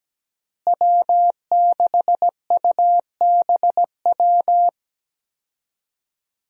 Morse code, heard as a keyed tone, about 17 wpm, 700 Hz. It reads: W6UBW
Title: USA Call Signs - 5 Characters - Repeated 3x - 17wpm